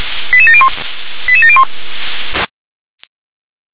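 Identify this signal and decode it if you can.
Five-tone selective call (ZVEI), heard as quick runs of five tones, tones of about 70 ms. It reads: 88701 88701
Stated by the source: ZVEI five-tone selective call received by SDR.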